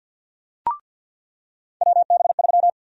Morse code, heard as a keyed tone, about 53 wpm, 700 Hz. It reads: WB3
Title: USA Call Sign Prefixes - 50wpm